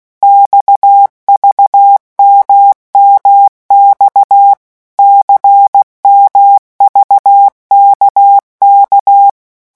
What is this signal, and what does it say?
Morse code, heard as a keyed tone, about 16 wpm, 800 Hz. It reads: XVMMX CMVKK
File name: morse mp3 / revision2-vitesse 15 mots